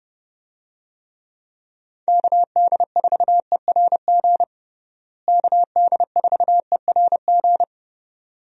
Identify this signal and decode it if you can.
Morse code, heard as a keyed tone, about 32 wpm, 700 Hz. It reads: KD4ERG KD4ERG
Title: USA Call Signs - 6 Characters - Repeated 2x - 30wpm